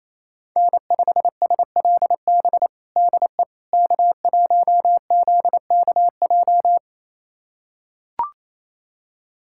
Morse code, heard as a keyed tone, about 27 wpm, 700 Hz. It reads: N5SLB DE K1ZKJ
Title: USA Callsign DE Callsign 28wpm